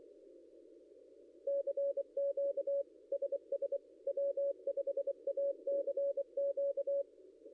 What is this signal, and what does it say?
Morse code, heard as a keyed tone, about 24 wpm, 560 Hz. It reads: CQ SS W5ACQ
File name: contests / ss-cw-k5tr-op-n5rz -0800